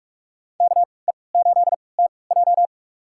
Morse code, heard as a keyed tone, about 46 wpm, spaced as for 25 wpm, 700 Hz. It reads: KE8TJ
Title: USA Call Signs - 5 Characters - 3x Character Spacing - 45wpm